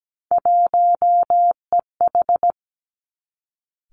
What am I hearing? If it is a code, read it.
Morse code, heard as a keyed tone, about 17 wpm, 700 Hz. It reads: 1EH
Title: Three Challenging Letter-Number Combinations 17wpm